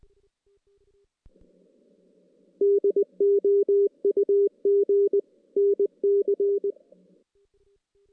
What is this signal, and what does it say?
Morse code, heard as a keyed tone, about 20 wpm, 410 Hz. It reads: DOUG NC
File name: 2016 JAN CW NAQP / 2016 Jan naqp k5ot - 0-5